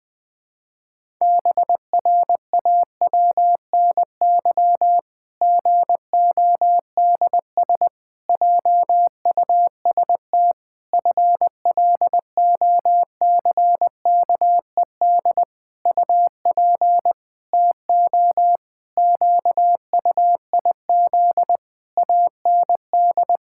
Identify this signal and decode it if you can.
Morse code, heard as a keyed tone, about 20 wpm, 700 Hz. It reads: BRAWNY GODS JUST FLOCKED UP TO QUIZ AND
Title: Pangram Challenge v2 - 20wpm